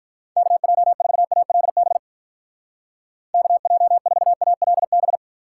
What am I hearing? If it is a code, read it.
Morse code, heard as a keyed tone, about 45 wpm, 700 Hz. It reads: KJ4ALB KJ4ALB